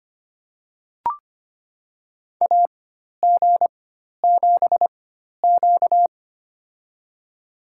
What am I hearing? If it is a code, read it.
Morse code, heard as a keyed tone, about 25 wpm, spaced as for 12 wpm, 700 Hz. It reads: AG7Q
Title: USA Call Signs - 4 Characters - 4x Character Spacing - 25wpm